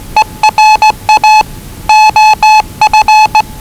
Morse code, heard as a keyed tone, about 20 wpm, 840 Hz. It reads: ERA OF
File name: WellingtonRadioZlwCloseDownOnHfAndMf / 01-Zlw 4277khz 417.5 qrt until 0925z